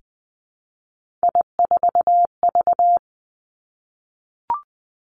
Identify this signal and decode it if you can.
Morse code, heard as a keyed tone, about 20 wpm, 700 Hz. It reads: I4V
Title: Three Challenging Letter-Number Combinations 20wpm